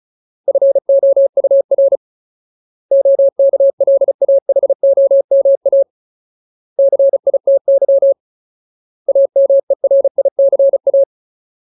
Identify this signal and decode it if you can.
Morse code, heard as a keyed tone, about 35 wpm, 550 Hz. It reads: FOUR OKLAHOMA CITY AMERICA
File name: Ultimate Word List / 35WPM Ultimate